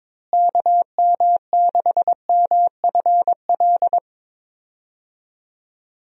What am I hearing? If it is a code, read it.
Morse code, heard as a keyed tone, about 22 wpm, 700 Hz. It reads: KM6MFL
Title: USA Call Signs - 6 Characters - Repeated 2x - 22wpm